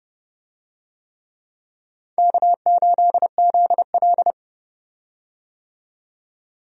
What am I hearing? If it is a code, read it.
Morse code, heard as a keyed tone, about 30 wpm, 700 Hz. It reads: K8ZL
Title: USA Call Signs - 4 Characters - 30wpm